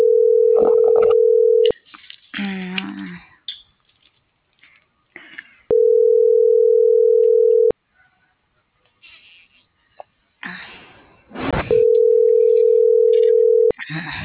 Ringback tone (North America), 440 + 480 Hz.